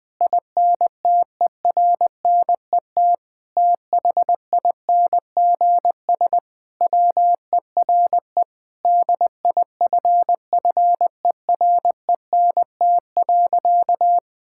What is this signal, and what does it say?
Morse code, heard as a keyed tone, about 20 wpm, 700 Hz. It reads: INTERNET THINGS WERE DIFFERENT.
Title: Sentences with 6-Word Phrases - 20wpm